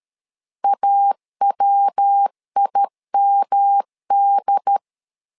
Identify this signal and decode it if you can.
Morse code, heard as a keyed tone, about 13 wpm, 800 Hz. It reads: AWIMD
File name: LEZIONI / 4 Q 12wpm 1833